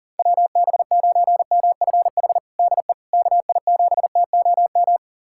Morse code, heard as a keyed tone, about 40 wpm, 700 Hz. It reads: WB9MFH DE KI7TOM
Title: USA Callsign DE Callsign - 6 Characters - 40wpm